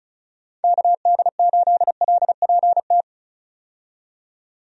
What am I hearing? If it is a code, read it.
Morse code, heard as a keyed tone, about 36 wpm, 700 Hz. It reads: KD8LPT
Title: USA Call Signs - 6 Characters - 35wpm